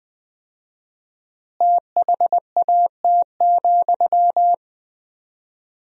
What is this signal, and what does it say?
Morse code, heard as a keyed tone, about 21 wpm, 700 Hz. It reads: THAT,
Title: 4x Word-Spacing - Sentences from Top 400 Words - 20wpm